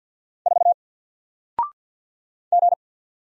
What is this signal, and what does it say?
Morse code, heard as a keyed tone, about 47 wpm, 700 Hz. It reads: 4 G
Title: Single Letter-Number 50wpm